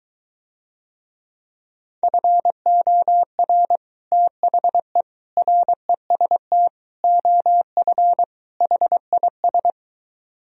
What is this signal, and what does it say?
Morse code, heard as a keyed tone, about 23 wpm, 700 Hz. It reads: FOR THE REST OF HIS